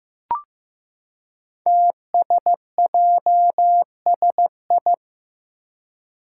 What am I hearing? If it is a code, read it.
Morse code, heard as a keyed tone, about 15 wpm, 700 Hz. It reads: TSJSI
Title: Five Challenging Letter-Number Combinations 15wpm